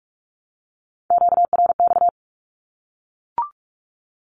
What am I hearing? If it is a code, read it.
Morse code, heard as a keyed tone, about 44 wpm, 700 Hz. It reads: QRX
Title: Q-Codes 45wpm